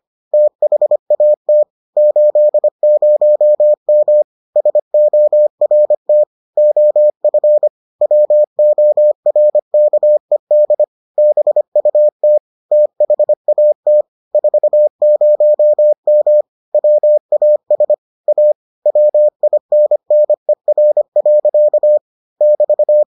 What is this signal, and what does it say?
Morse code, heard as a keyed tone, about 25 wpm, 600 Hz. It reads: THAT 80M SORT OF WORKED BUT THAT 40M WAS A WINNER. <BT>